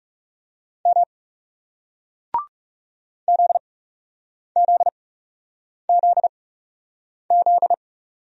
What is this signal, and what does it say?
Morse code, heard as a keyed tone, about 40 wpm, spaced as for 8 wpm, 700 Hz. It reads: M ZZZZ